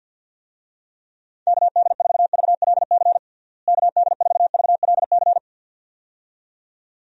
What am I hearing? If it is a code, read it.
Morse code, heard as a keyed tone, about 47 wpm, 700 Hz. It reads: KD4VLC KD4VLC
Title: USA Call Signs - 6 Characters - Repeated 2x - 50wpm